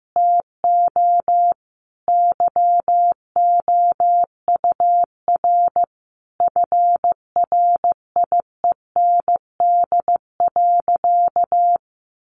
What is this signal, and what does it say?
Morse code, heard as a keyed tone, about 15 wpm, 700 Hz. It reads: TO YOUR FRIEND.
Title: Sentences from Top 1300 Words 15wpm